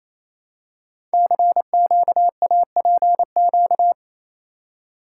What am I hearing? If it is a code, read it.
Morse code, heard as a keyed tone, about 28 wpm, 700 Hz. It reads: CQAPQ
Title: Code Groups - 5 Letters - 28wpm